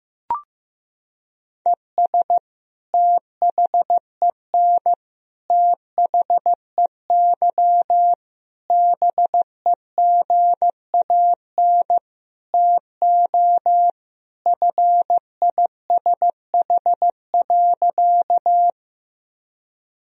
Morse code, heard as a keyed tone, about 15 wpm, 700 Hz. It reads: ES THEN THEY BEGAN TO FISH.
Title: Sentences from Top 300 Words and Abbreviations 15wpm